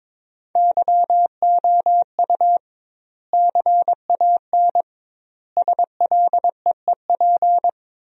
Morse code, heard as a keyed tone, about 22 wpm, 700 Hz. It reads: YOU CAN SLEEP
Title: 2x Word-Spacing - Sentences from Top 500 Words - 22wpm